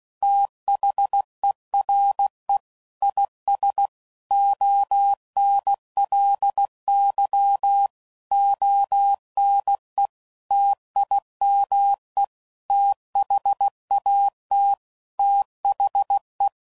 Morse code, heard as a keyed tone, about 16 wpm, 800 Hz. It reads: THERE IS ONLY ONE TIME THAT THE
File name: CW Code Groups @15WPM / Technical Subjects / OR Gate